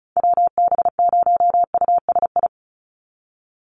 Morse code, heard as a keyed tone, about 34 wpm, 700 Hz. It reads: WB0USI
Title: NA SOTA Chaser Callsigns 35wpm